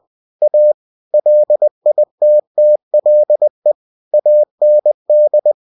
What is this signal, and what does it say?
Morse code, heard as a keyed tone, about 20 wpm, 600 Hz. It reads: A LITTLE AND
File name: cw-books / winnie-the-pooh / winnie-20wpm-page-0089